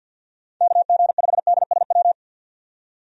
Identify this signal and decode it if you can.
Morse code, heard as a keyed tone, about 50 wpm, 700 Hz. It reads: KG5DSW
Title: USA Call Signs - 6 Characters - 50wpm